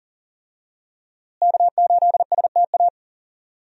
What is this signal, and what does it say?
Morse code, heard as a keyed tone, about 40 wpm, 700 Hz. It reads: K8STA
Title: USA Call Signs - 5 Characters - Repeated 2x - 40wpm